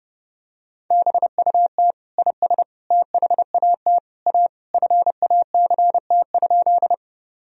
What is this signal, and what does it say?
Morse code, heard as a keyed tone, about 30 wpm, 700 Hz. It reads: BUT IS THAT A FACT?
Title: Sentences from Top 1000 Words - Review Part 3 - 30wpm